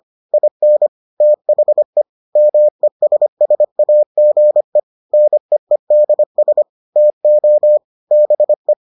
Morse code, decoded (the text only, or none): IN THE MESSAGE NEEDS TO BE